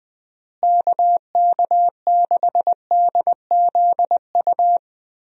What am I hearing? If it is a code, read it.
Morse code, heard as a keyed tone, about 20 wpm, 700 Hz. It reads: KK6DZU
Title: USA Call Signs - 6 Characters - Repeated 3x - 20wpm